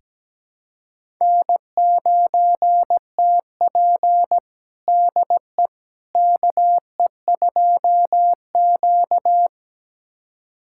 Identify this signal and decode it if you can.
Morse code, heard as a keyed tone, about 17 wpm, 700 Hz. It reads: N9TP DE KE2Q
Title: USA Callsign DE Callsign - 4 Characters - 17wpm